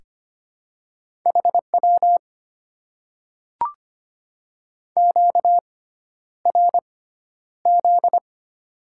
Morse code, heard as a keyed tone, about 25 wpm, spaced as for 11 wpm, 700 Hz. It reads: HW QRZ
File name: Top 90 QSO Elements - 6x Character Spacing - 25wpm